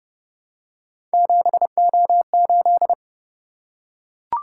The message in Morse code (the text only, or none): 7O8